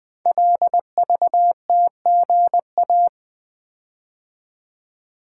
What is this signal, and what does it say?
Morse code, heard as a keyed tone, about 20 wpm, 700 Hz. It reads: LVTGA
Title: Code Groups - 5 Letters - 20wpm